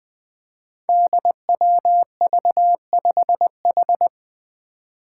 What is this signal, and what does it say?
Morse code, heard as a keyed tone, about 20 wpm, 700 Hz. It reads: DWV5H
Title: Five Challenging Letter-Number Combinations 20wpm